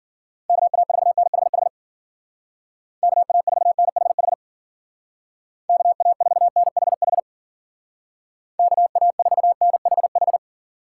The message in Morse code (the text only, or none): KA4NHH KA4NHH KA4NHH KA4NHH